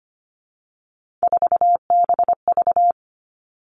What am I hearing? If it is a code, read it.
Morse code, heard as a keyed tone, about 25 wpm, 700 Hz. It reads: 4BV
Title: Three Challenging Letter-Number Combinations 25wpm